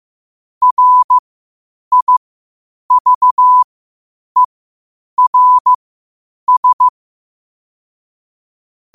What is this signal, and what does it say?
Morse code, heard as a keyed tone, about 15 wpm, spaced as for 8 wpm, 1000 Hz. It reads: RIVERS
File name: qotdfiles / 8wpm